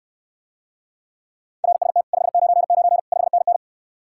Morse code, heard as a keyed tone, about 69 wpm, 700 Hz. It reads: RST 599 5NN